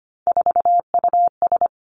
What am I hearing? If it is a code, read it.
Morse code, heard as a keyed tone, about 25 wpm, 700 Hz. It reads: 4US
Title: Three Challenging Letter-Number Combinations 25wpm